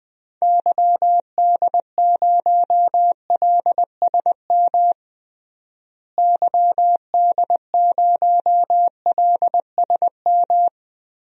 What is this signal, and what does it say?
Morse code, heard as a keyed tone, about 20 wpm, 700 Hz. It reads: YD0LSM YD0LSM
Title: DXCC-100 Call Signs - 6 Characters - Repeated 3x - 20wpm